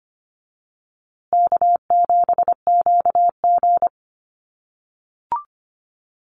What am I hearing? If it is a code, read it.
Morse code, heard as a keyed tone, about 25 wpm, 700 Hz. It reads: K7QG
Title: USA Call Signs Repeated 3x 25wpm